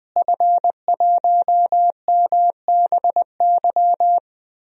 Morse code, heard as a keyed tone, about 20 wpm, 700 Hz. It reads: F1MBY
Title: DXCC-100 Call Signs - 5 Characters - Repeated 3x - 20wpm